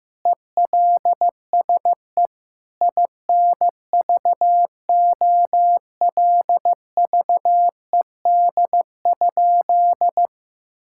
Morse code, heard as a keyed tone, about 15 wpm, 700 Hz. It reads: ELSE INVOLVED?
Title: Sentences from Top 1000 Words - Review Part 5 - 15wpm